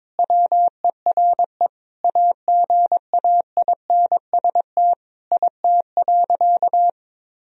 Morse code, heard as a keyed tone, about 22 wpm, 700 Hz. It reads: WERE AGAINST IT.
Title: Sentences from Top 1000 Words - Review Part 1 - 22wpm